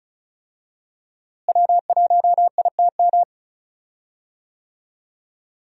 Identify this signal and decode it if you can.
Morse code, heard as a keyed tone, about 36 wpm, 700 Hz. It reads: W1ITM
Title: USA Call Signs - 5 Characters - Repeated 2x - 35wpm